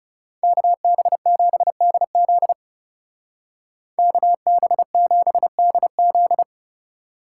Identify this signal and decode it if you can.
Morse code, heard as a keyed tone, about 36 wpm, 700 Hz. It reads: KB7DZ KB7DZ
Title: Speed Racing - USA 2x2 Call Signs 30wpm